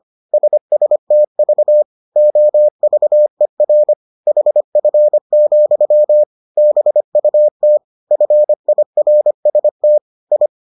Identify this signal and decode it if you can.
Morse code, heard as a keyed tone, about 25 wpm, 600 Hz. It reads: SSTV OVER HF, BUT FIRST I